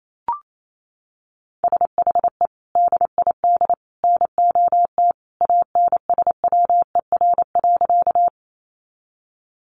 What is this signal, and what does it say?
Morse code, heard as a keyed tone, about 28 wpm, 700 Hz. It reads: SHE DID NOT ANSWER.
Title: Sentences from Top 600 Words 28wpm